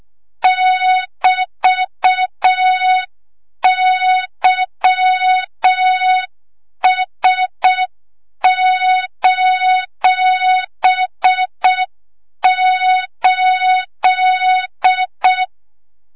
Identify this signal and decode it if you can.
Morse code, heard as a keyed tone, about 6 wpm, 3700 Hz, with noise. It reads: <BT>YS:8
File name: Morze-HA2MN / hang / 25 GYAKORLAT